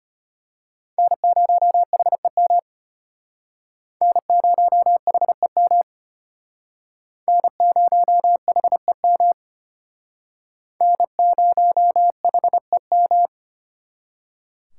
Morse code, heard as a keyed tone, about 31 wpm, 700 Hz. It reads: N0HEM N0HEM N0HEM N0HEM
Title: Speed Racing - USA 1x3 Call Signs 25wpm